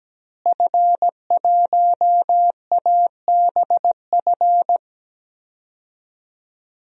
Morse code, heard as a keyed tone, about 17 wpm, 700 Hz. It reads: F1ABF
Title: DXCC-100 Call Signs - 5 Characters - Repeated 3x - 17wpm